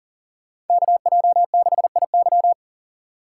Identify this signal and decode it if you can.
Morse code, heard as a keyed tone, about 52 wpm, 700 Hz. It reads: KJ6IY